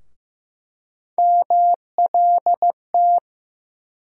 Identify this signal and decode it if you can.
Morse code, heard as a keyed tone, about 15 wpm, 700 Hz. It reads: MLT
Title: USA Call Sign Postfixes - 3 Letters 15wpm